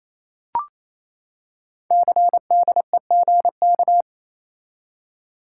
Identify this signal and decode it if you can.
Morse code, heard as a keyed tone, about 27 wpm, 700 Hz. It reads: CDEGK
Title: Code Groups - 5 Letters - 28wpm